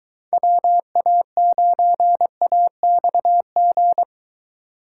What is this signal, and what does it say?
Morse code, heard as a keyed tone, about 23 wpm, 700 Hz. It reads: WA9AXG